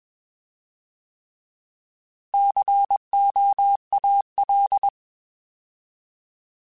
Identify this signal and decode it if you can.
Morse code, heard as a keyed tone, about 21 wpm, 800 Hz. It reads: COAL